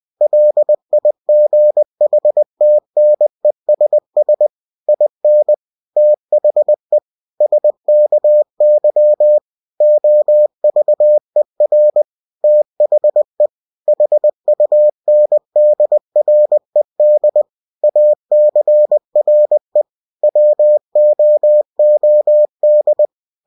Morse code, heard as a keyed tone, about 20 wpm, 600 Hz. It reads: LIGHTNESS IN THE SKY OVER THE HUNDRED ACRE WOOD
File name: cw-books / winnie-the-pooh / winnie-20wpm-page-0040